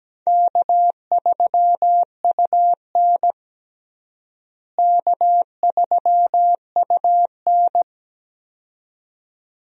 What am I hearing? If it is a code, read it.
Morse code, heard as a keyed tone, about 17 wpm, 700 Hz. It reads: K3UN K3UN
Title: USA Call Signs - 4 Characters - Repeated 2x - 17wpm